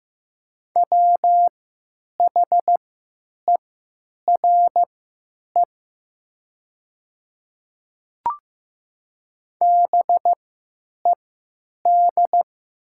Morse code, heard as a keyed tone, about 15 wpm, spaced as for 9 wpm, 700 Hz. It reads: WHERE BED
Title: 3x Character Spacing - Top 500 Words - 15wpm